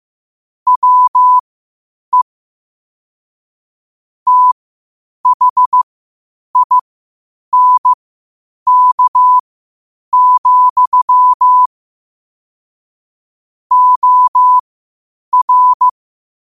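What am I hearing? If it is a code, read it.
Morse code, heard as a keyed tone, about 15 wpm, spaced as for 8 wpm, 1000 Hz. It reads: WE THINK, OR
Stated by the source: Morse practice at 8 WPM from Morse Resource.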